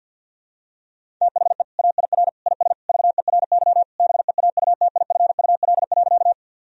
Morse code, heard as a keyed tone, about 46 wpm, 700 Hz. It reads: THE AIR IS VERY BEAUTIFUL.
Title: Sentences from Top 1000 Words - Review Part 1 - 50wpm